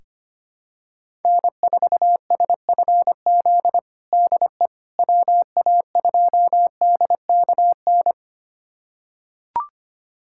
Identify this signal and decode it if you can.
Morse code, heard as a keyed tone, about 25 wpm, 700 Hz. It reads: N4SFZ DE WA2DKN
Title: USA Callsign DE Callsign 25wpm